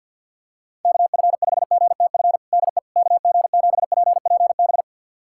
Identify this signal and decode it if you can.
Morse code, heard as a keyed tone, about 48 wpm, 700 Hz. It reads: KF5GTF DE KG7PPB
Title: USA Callsign DE Callsign 50wpm